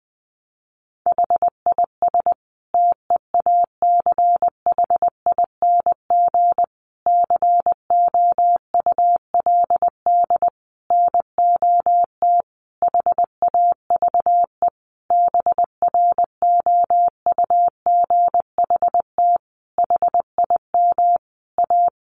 Morse code, heard as a keyed tone, about 20 wpm, 700 Hz. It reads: HIS TEACHING COULD NOT HAVE BROUGHT HIM A
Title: Sentences from Top 1100 Words 20wpm